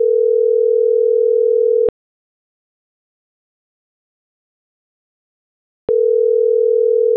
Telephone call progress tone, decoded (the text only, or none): ringback tone